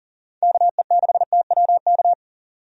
Morse code, heard as a keyed tone, about 40 wpm, 700 Hz. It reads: KE6TWK
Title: USA Call Signs - 6 Characters - 40wpm